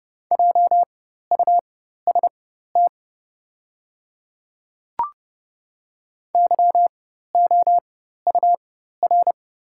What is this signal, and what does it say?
Morse code, heard as a keyed tone, about 30 wpm, spaced as for 14 wpm, 700 Hz. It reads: JUST YOUR
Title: 4x Character Spacing - Top 300 Words - 30wpm